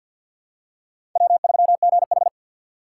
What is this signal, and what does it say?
Morse code, heard as a keyed tone, about 52 wpm, 700 Hz. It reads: W3GH
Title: USA Call Signs - 4 Characters - 3x Character Spacing - 50wpm